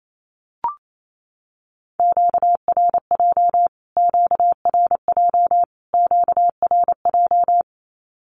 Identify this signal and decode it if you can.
Morse code, heard as a keyed tone, about 28 wpm, 700 Hz. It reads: QRJ QRJ QRJ
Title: Q-Codes Repeated 3x 28wpm